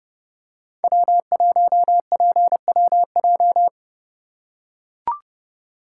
Morse code, heard as a keyed tone, about 32 wpm, 700 Hz. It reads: W1PWJ
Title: USA Call Signs - 5 Characters - 4x Character Spacing - 30wpm